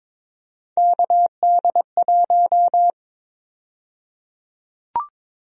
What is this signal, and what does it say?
Morse code, heard as a keyed tone, about 22 wpm, 700 Hz. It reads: KD1
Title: USA Call Sign Prefixes - Repeated 2x - 22wpm